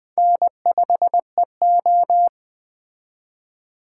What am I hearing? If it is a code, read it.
Morse code, heard as a keyed tone, about 20 wpm, 700 Hz. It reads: N5EO